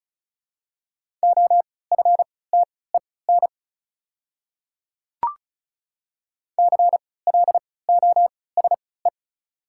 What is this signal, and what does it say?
Morse code, heard as a keyed tone, about 36 wpm, spaced as for 20 wpm, 700 Hz. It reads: OFTEN CLOSE
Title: 3x Character Spacing - Top 300 Words - 35wpm